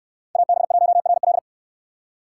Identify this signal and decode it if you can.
Morse code, heard as a keyed tone, about 64 wpm, 700 Hz. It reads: AB1RL